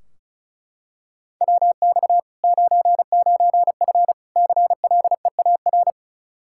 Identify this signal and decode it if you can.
Morse code, heard as a keyed tone, about 36 wpm, 700 Hz. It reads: WX 99F CLEAR
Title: Practice QSO - Exchange 2 - Info - 35wpm